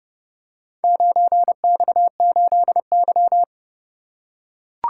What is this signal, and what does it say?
Morse code, heard as a keyed tone, about 30 wpm, 700 Hz. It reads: 9X8Y